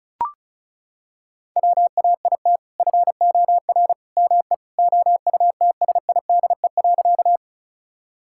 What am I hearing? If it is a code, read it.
Morse code, heard as a keyed tone, about 36 wpm, 700 Hz. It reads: WAIT FOR ME OUTSIDE.
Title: Sentences from Top 1000 Words - Review Part 6 - 35wpm